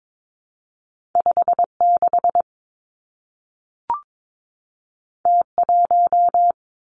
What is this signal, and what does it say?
Morse code, heard as a keyed tone, about 22 wpm, 700 Hz. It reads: 5 6 T 1
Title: Two Challenging Letter-Number Combinations 22wpm